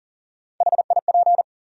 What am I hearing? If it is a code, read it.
Morse code, heard as a keyed tone, about 40 wpm, 700 Hz. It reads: HIP
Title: USA Call Sign Postfixes - 3 Letters 40wpm